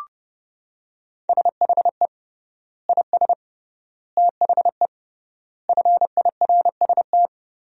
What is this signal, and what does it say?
Morse code, heard as a keyed tone, about 30 wpm, 700 Hz. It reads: SHE IS THE FIRST
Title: 3x Word-Spacing - Sentences from Top 400 Words - 30wpm